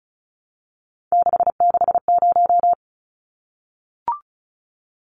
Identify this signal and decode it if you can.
Morse code, heard as a keyed tone, about 34 wpm, 700 Hz. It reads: 660
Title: Numbers - 3-Digit 35wpm